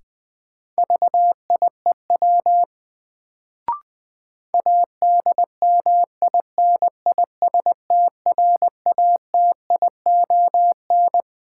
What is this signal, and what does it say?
Morse code, heard as a keyed tone, about 20 wpm, 700 Hz. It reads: VIEW ADMINISTRATION
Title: Sets of 5 Words - New Word Drill - 20wpm